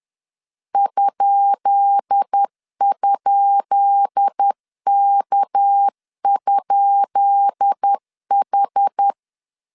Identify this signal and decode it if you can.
Morse code, heard as a keyed tone, about 10 wpm, 800 Hz. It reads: ??K?H
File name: LEZIONI / 10 SIGNS / 10B INT 10wpm 2844